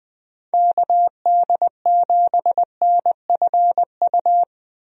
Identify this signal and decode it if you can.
Morse code, heard as a keyed tone, about 20 wpm, 700 Hz. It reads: KD7NFU
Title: USA Call Signs - 6 Characters - Repeated 2x - 20wpm